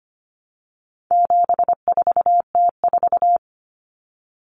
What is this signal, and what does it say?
Morse code, heard as a keyed tone, about 25 wpm, 700 Hz. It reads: 74T4
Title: Common Cut Numbers - 4-Digit 25wpm